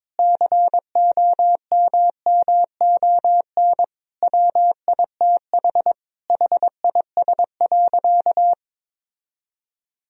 Morse code, heard as a keyed tone, about 22 wpm, 700 Hz. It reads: COMMON WITH HIS.
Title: Sentences from Top 1000 Words - Review Part 3 - 22wpm